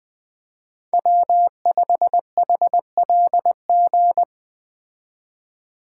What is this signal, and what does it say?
Morse code, heard as a keyed tone, about 20 wpm, 700 Hz. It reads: W5HLG
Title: Speed Racing - USA 1x3 Call Signs 17wpm